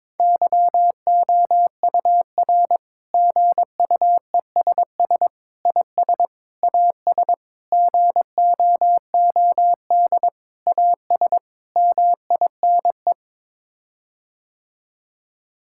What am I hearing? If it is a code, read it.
Morse code, heard as a keyed tone, about 21 wpm, 700 Hz. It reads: YOUR GUESS IS AS GOOD AS MINE
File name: practice / 22wpm / Sets of 7 Words - 22wpm